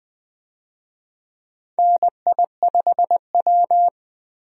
Morse code, heard as a keyed tone, about 20 wpm, 700 Hz. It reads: NI5W
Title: USA Call Signs - 4 Characters - 20wpm